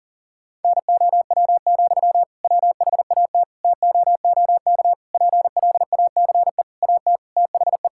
Morse code, heard as a keyed tone, about 37 wpm, 700 Hz. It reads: NOW, WHAT TOOK PLACE AT THE
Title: Sentences from Top 1600 Words 40wpm